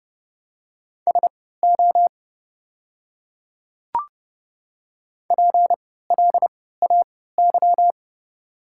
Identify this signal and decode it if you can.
Morse code, heard as a keyed tone, about 30 wpm, spaced as for 17 wpm, 700 Hz. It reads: SO PLAY
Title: 3x Character Spacing - Top 200 Words - 30wpm